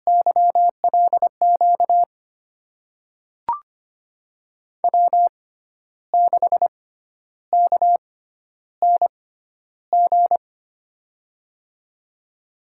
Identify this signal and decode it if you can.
Morse code, heard as a keyed tone, about 25 wpm, spaced as for 11 wpm, 700 Hz. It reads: YLQ W6KNG
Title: USA Call Signs - 6x Character Spacing - 25wpm